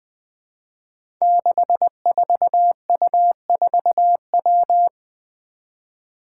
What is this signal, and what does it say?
Morse code, heard as a keyed tone, about 20 wpm, 700 Hz. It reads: 64U4W
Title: Five Challenging Letter-Number Combinations 20wpm